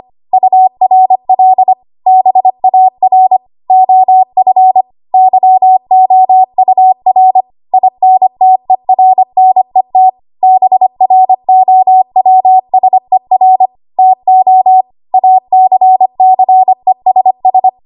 Morse code, heard as a keyed tone, about 25 wpm, 750 Hz. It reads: URL BAR OF YOUR INTERNET BROWSER TO ACCESS
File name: Archive / 25 WPM / 25WPM